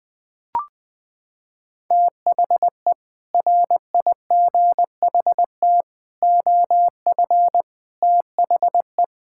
Morse code, heard as a keyed tone, about 21 wpm, 700 Hz. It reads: THE RIGHT OF THE
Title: Sets of 6 Words - 20wpm